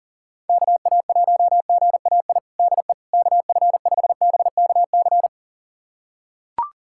Morse code, heard as a keyed tone, about 40 wpm, 700 Hz. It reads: KA1GAI DE KF5BKC
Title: USA Callsign DE Callsign - 6 Characters - 40wpm